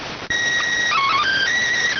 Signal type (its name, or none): SSTV (slow-scan television)